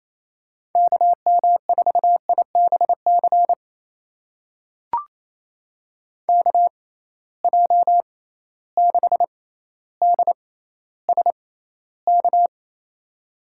Morse code, heard as a keyed tone, about 28 wpm, spaced as for 14 wpm, 700 Hz. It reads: KM4IBC KJ6DSK
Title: USA Call Signs - 6 Characters - 6x Character Spacing - 28wpm